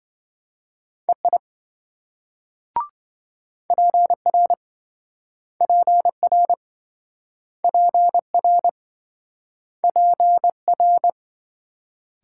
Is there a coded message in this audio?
Yes: Morse code from 1.1 to 11.1 s, heard as a keyed tone, about 26 wpm, 700 Hz. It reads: EI PR PR PR PR